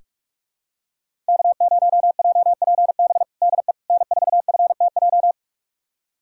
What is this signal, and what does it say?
Morse code, heard as a keyed tone, about 46 wpm, 700 Hz. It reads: K0JPB DE N4FTJ
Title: USA Callsign DE Callsign - 5 Characters - 45wpm